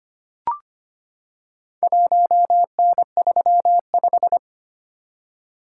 Morse code, heard as a keyed tone, about 25 wpm, 700 Hz. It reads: 1N35
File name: practice / 25wpm / Four Letter-Number Combinations - 25wpm